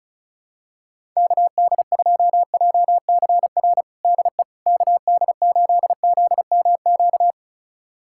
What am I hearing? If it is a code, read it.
Morse code, heard as a keyed tone, about 34 wpm, 700 Hz. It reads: KD2JCR DE KD8ZMQ
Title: USA Callsign DE Callsign 35wpm